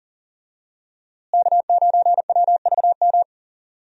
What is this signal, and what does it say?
Morse code, heard as a keyed tone, about 40 wpm, 700 Hz. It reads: K9WVM
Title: USA Call Signs - 5 Characters - Repeated 3x - 40wpm